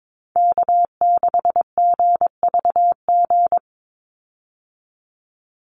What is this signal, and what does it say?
Morse code, heard as a keyed tone, about 22 wpm, 700 Hz. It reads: K6GVG
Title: Call Signs Repeated 3x - 22wpm